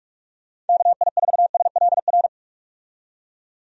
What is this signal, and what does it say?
Morse code, heard as a keyed tone, about 46 wpm, 700 Hz. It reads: KI4SLR
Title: USA Call Signs - 6 Characters - 45wpm